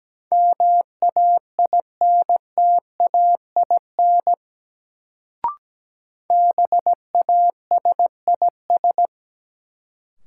Morse code, heard as a keyed tone, about 17 wpm, 700 Hz. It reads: MAINTAIN BASIS
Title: Sentences from Top 1000 Words - Review Part 5 - New Word Drill - 17wpm